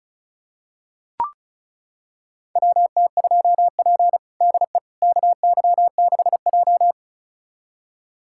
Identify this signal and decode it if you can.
Morse code, heard as a keyed tone, about 36 wpm, 700 Hz. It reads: WT2P DE KY6J
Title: USA Callsign DE Callsign - 4 Characters - 35wpm